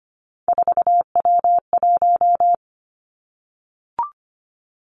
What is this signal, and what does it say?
Morse code, heard as a keyed tone, about 25 wpm, 700 Hz. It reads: 4W1
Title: Three Challenging Letter-Number Combinations 25wpm